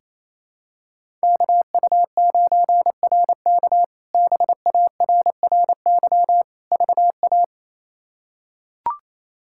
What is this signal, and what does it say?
Morse code, heard as a keyed tone, about 28 wpm, 700 Hz. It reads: KU9RK BARRY VA
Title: K1USN SST QSO Practice - 28wpm